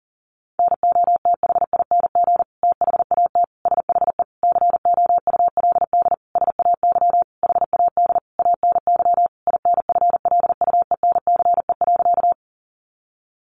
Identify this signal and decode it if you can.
Morse code, heard as a keyed tone, about 37 wpm, 700 Hz. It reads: NOTHING THAT SHE COULD SAY HAD ANY INFLUENCE.
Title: Sentences from Top 1100 Words 40wpm